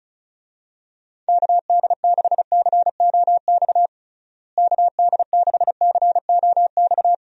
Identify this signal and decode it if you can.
Morse code, heard as a keyed tone, about 36 wpm, 700 Hz. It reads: KD6COX KD6COX
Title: USA Call Signs - 6 Characters - Repeated 2x - 35wpm